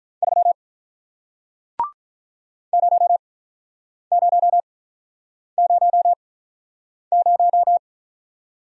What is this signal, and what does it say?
Morse code, heard as a keyed tone, about 48 wpm, spaced as for 9 wpm, 700 Hz. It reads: 3 0000